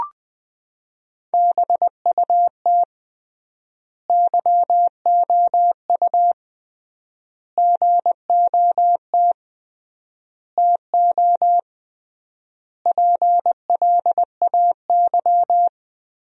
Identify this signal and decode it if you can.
Morse code, heard as a keyed tone, about 20 wpm, 700 Hz. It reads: BUT YOU GOT TO PLAY
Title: 3x Word-Spacing - Sentences from Top 300 Words - 20wpm